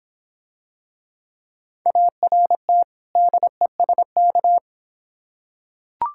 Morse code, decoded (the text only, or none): ART DESK